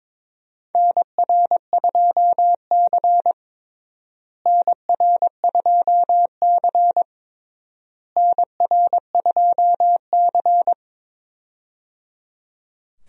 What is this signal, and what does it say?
Morse code, heard as a keyed tone, about 22 wpm, 700 Hz. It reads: NR2C NR2C NR2C
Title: USA Call Signs - 4 Characters - Repeated 3x - 22wpm